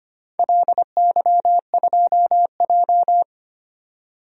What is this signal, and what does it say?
Morse code, heard as a keyed tone, about 25 wpm, 700 Hz. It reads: LY2J
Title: DXCC-100 Call Signs - 4 Characters - 25wpm